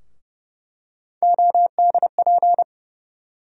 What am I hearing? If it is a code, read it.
Morse code, heard as a keyed tone, about 30 wpm, 700 Hz. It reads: ODP
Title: USA Call Sign Postfixes - 3 Letters 30wpm